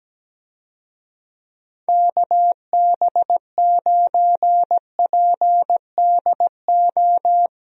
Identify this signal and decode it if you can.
Morse code, heard as a keyed tone, about 17 wpm, 700 Hz. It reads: KB9PDO
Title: USA Call Signs - 6 Characters - Repeated 3x - 17wpm